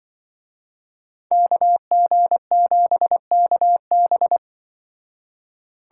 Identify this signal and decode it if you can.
Morse code, heard as a keyed tone, about 25 wpm, 700 Hz. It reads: KG7KB
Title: Speed Racing - USA 2x2 Call Signs 20wpm